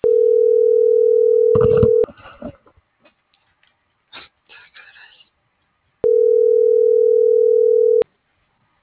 Ringback tone (North America), 440 + 480 Hz.